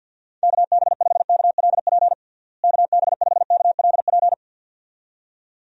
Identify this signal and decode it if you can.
Morse code, heard as a keyed tone, about 51 wpm, 700 Hz. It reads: KB5KLP KB5KLP
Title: USA Call Signs - 6 Characters - Repeated 2x - 50wpm